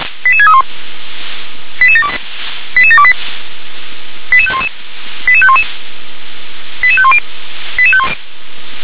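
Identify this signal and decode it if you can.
Five-tone selective call (ZVEI), heard as quick runs of five tones, tones of about 80 ms. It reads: 88631 88728 88511 88511 88510
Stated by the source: ZVEI five-tone call received by SDR.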